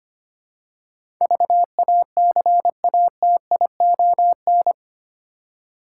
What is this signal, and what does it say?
Morse code, heard as a keyed tone, about 25 wpm, 700 Hz. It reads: VACATION
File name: 8 Letter Words - First 3 Letters Given - 25wpm